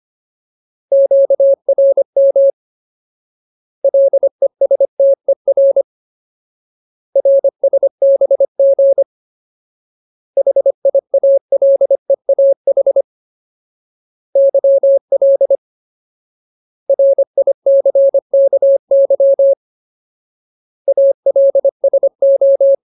Morse code, heard as a keyed tone, about 25 wpm, 550 Hz. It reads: QRM LESTER RSBG HIALEAH YL RICKY ALSO
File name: Ultimate Word List / 25WPM Ultimate